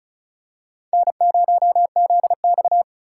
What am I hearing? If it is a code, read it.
Morse code, heard as a keyed tone, about 34 wpm, 700 Hz. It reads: N0ZX
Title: USA Call Signs - 4 Characters - 4x Character Spacing - 35wpm